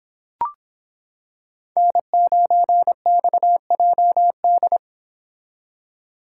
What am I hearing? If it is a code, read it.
Morse code, heard as a keyed tone, about 26 wpm, 700 Hz. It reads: N9XJD